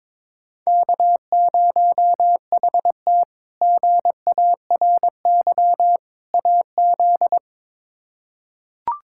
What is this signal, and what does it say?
Morse code, heard as a keyed tone, about 21 wpm, 700 Hz. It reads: K0HT GARY AZ
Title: K1USN SST QSO Practice - 22wpm